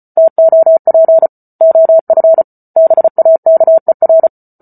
Morse code, heard as a keyed tone, about 34 wpm, 650 Hz. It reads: TOP OF BAKER